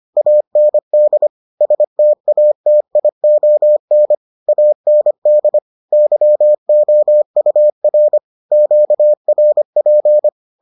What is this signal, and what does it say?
Morse code, heard as a keyed tone, about 25 wpm, 600 Hz. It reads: AND STATION AND YOUR QRP